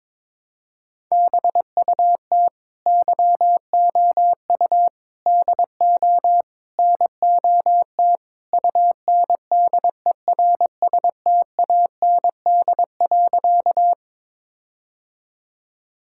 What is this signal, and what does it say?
Morse code, heard as a keyed tone, about 22 wpm, 700 Hz. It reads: BUT YOU DO NOT UNDERSTAND.
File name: practice / 22wpm / Sentences with 4-Word Phrases - 22wpm